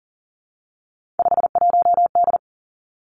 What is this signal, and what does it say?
Morse code, heard as a keyed tone, about 42 wpm, 700 Hz. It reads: 51D
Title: Three Challenging Letter-Number Combinations 40wpm